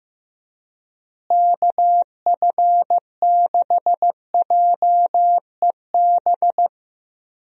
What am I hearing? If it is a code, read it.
Morse code, heard as a keyed tone, about 16 wpm, 700 Hz. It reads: KF6JEB
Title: USA Call Signs - 6 Characters - Repeated 2x - 15wpm